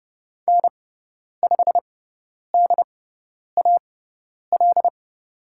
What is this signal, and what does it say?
Morse code, heard as a keyed tone, about 30 wpm, spaced as for 10 wpm, 700 Hz. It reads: N5DAL
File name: Lesson 16g - Letter c - Call Signs 30wpm